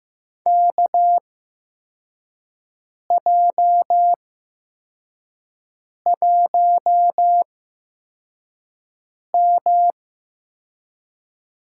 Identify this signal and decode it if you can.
Morse code, heard as a keyed tone, about 15 wpm, spaced as for 4 wpm, 700 Hz. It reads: KJ1M
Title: USA Call Signs - 4 Characters - 8x Character Spacing - 15wpm